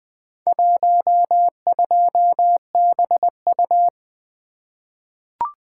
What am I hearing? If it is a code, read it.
Morse code, heard as a keyed tone, about 20 wpm, 700 Hz. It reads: 12BU
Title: Four Letter-Number Combinations - 20wpm